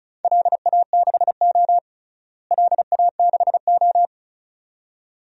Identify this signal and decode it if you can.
Morse code, heard as a keyed tone, about 34 wpm, 700 Hz. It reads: LA6O LA6O
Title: DXCC-100 Call Signs - 4 Characters - Repeated 2x - 35wpm